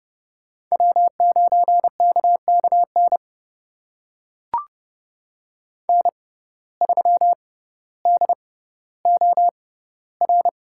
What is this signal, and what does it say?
Morse code, heard as a keyed tone, about 30 wpm, spaced as for 15 wpm, 700 Hz. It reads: W9KKN N3DOR
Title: USA Call Signs - 5 Characters - 6x Character Spacing - 30wpm